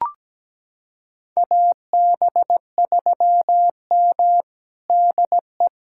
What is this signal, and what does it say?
Morse code, heard as a keyed tone, about 17 wpm, 700 Hz. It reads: AB3M DE
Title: USA Callsign DE Callsign - 4 Characters - 17wpm